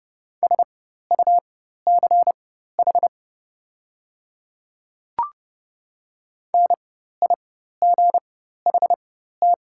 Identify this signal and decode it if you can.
Morse code, heard as a keyed tone, about 30 wpm, spaced as for 14 wpm, 700 Hz. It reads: SUCH NIGHT
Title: 4x Character Spacing - Top 300 Words - 30wpm